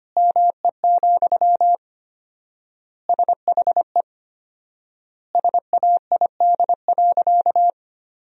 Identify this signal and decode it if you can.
Morse code, heard as a keyed tone, about 25 wpm, 700 Hz. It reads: ME, SHE SAID.
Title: 4x Word-Spacing - Sentences from Top 500 Words - 25wpm